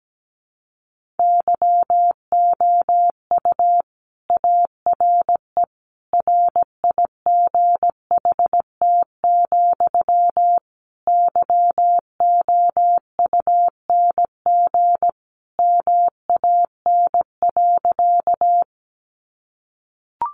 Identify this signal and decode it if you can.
Morse code, heard as a keyed tone, about 17 wpm, 700 Hz. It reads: YOU ARE RIGHT, YOUNG MAN.
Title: Sentences from Top 400 Words 17wpm